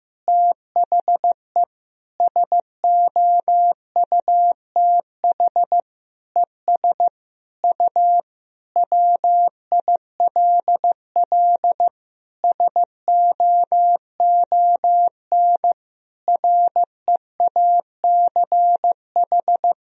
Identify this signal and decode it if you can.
Morse code, heard as a keyed tone, about 15 wpm, 700 Hz. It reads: THE SOUTH ES U WILL SOON REACH
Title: Sentences from Top 400 Words and Abbreviations 15wpm